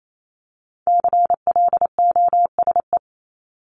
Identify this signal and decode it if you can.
Morse code, heard as a keyed tone, about 28 wpm, 700 Hz. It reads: CLOSE
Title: LY Suffix - Top 100 Words with Base Words 28wpm